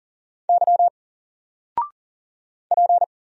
Morse code, heard as a keyed tone, about 38 wpm, 700 Hz. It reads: Y P
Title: Single Character (Letter:Number:Punctuation:Prosign) 40wpm